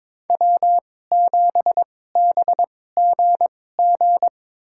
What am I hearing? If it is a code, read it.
Morse code, heard as a keyed tone, about 22 wpm, spaced as for 16 wpm, 700 Hz. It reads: W7BGG
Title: USA Call Signs - 5 Characters - 2x Character Spacing - 22wpm